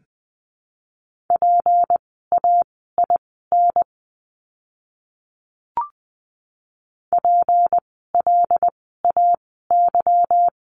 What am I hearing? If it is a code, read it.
Morse code, heard as a keyed tone, about 20 wpm, spaced as for 15 wpm, 700 Hz. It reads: PAIN PLAY
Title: 153 Lesson 25f - Letter p - Words 20wpm